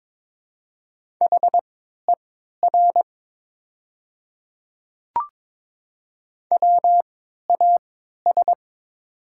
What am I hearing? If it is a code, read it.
Morse code, heard as a keyed tone, about 22 wpm, spaced as for 12 wpm, 700 Hz. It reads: HER WAS
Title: 3x Character Spacing - Top 100 Words - 22wpm